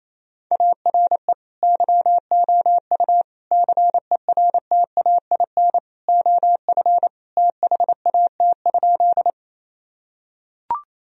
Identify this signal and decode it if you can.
Morse code, heard as a keyed tone, about 28 wpm, 700 Hz. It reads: ARE YOU CERTAIN OF THAT?
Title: Sentences from Top 1000 Words - Review Part 2 - 28wpm